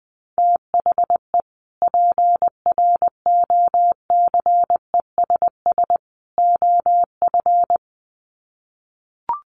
Morse code, heard as a keyed tone, about 20 wpm, 700 Hz. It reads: THE PROCESS OF
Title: Sets of 3 Words 20wpm